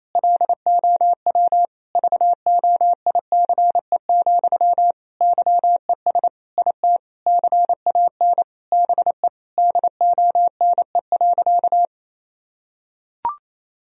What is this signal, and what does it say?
Morse code, heard as a keyed tone, about 28 wpm, 700 Hz. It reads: LOW VOICE, YES IT CAN BE DONE.
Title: Sentences from Top 1000 Words - Review Part 1 - 28wpm